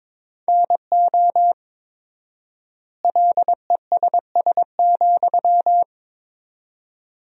Morse code, heard as a keyed tone, about 22 wpm, 700 Hz. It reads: NO LESS,
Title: 4x Word-Spacing - Sentences from Top 400 Words - 22wpm